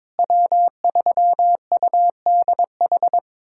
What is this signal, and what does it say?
Morse code, heard as a keyed tone, about 22 wpm, 700 Hz. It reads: W3UDH
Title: USA Callsign DE Callsign - 5 Characters - 22wpm